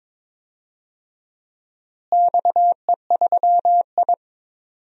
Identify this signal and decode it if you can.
Morse code, heard as a keyed tone, about 22 wpm, 700 Hz. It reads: XE3I
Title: DXCC-100 Call Signs - 4 Characters - Repeated 2x - 22wpm